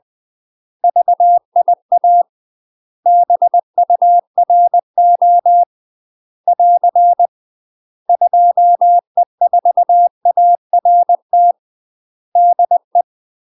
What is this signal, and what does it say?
Morse code, heard as a keyed tone, about 20 wpm, 700 Hz. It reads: VIA BURO <AR> 2E4ART DE
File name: UK MASTER QSOs TRAINING/20WPM